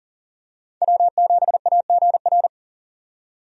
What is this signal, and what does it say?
Morse code, heard as a keyed tone, about 40 wpm, 700 Hz. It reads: W7AGR
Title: USA Call Signs - 5 Characters - 40wpm